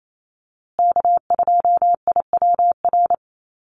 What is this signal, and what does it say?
Morse code, heard as a keyed tone, about 28 wpm, 700 Hz. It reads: K2IWR
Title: USA Call Signs Repeated 3x 28wpm